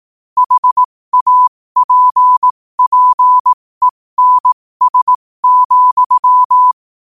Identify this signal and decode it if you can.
Morse code, heard as a keyed tone, about 18 wpm, spaced as for 15 wpm, 1000 Hz. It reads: HAPPENS,
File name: qotdfiles / 15wpm